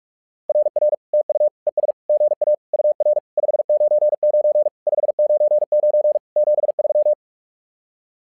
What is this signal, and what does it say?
Morse code, heard as a keyed tone, about 45 wpm, 600 Hz. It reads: RR TU ES GA UR 599 599 73
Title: SOTA QSO Practice 45wpm